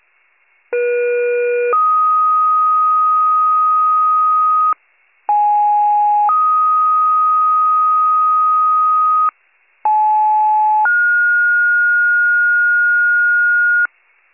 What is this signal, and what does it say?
Two-tone page, heard as a one-second tone, then a three-second tone, about 480 then 1250 Hz, about 830 then 1250 Hz, about 860 then 1430 Hz. It is heard from 0.7 to 13.9 s.